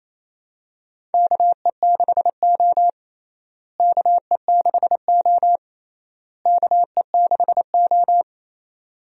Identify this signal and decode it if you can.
Morse code, heard as a keyed tone, about 28 wpm, 700 Hz. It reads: KE6O KE6O KE6O
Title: USA Call Signs - 4 Characters - Repeated 3x - 28wpm